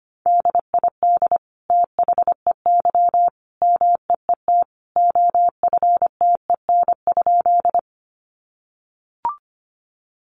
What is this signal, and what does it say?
Morse code, heard as a keyed tone, about 25 wpm, 700 Hz. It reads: DID THEY MEET OFTEN?
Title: Sentences from Top 1100 Words 25wpm